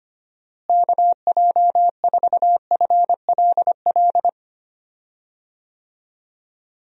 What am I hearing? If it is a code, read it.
Morse code, heard as a keyed tone, about 25 wpm, 700 Hz. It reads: KJ4FLL
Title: USA Call Signs - 6 Characters - Repeated 3x - 25wpm